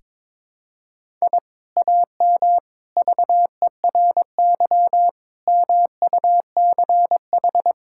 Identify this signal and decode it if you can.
Morse code, heard as a keyed tone, about 22 wpm, 700 Hz. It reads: I AM VERY MUCH
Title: Sentences from Top 1000 Words - Review Part 6 - 22wpm